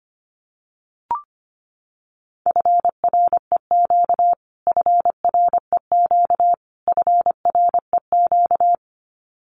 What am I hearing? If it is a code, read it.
Morse code, heard as a keyed tone, about 25 wpm, 700 Hz. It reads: FREQ FREQ FREQ
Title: Abbreviations Repeated 3x - 25wpm